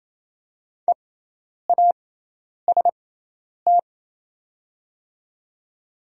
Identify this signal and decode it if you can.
Morse code, heard as a keyed tone, about 28 wpm, spaced as for 10 wpm, 700 Hz. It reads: EAST
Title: 6x Character Spacing - Top 500 Words - 28wpm